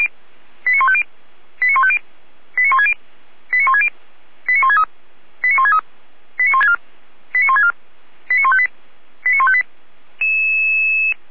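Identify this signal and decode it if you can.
Five-tone selective call (ZVEI), heard as quick runs of five tones, tones of about 70 ms. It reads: 89150 89150 89160 89160 89163 89163 89164 89164 89158 89158
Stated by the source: Five-tone ZVEI selective call received on an SDR.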